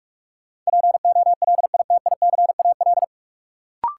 Morse code, heard as a keyed tone, about 43 wpm, 700 Hz. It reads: POLITICAL
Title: Sentences from Top 800 Words - New Word Drill - 45wpm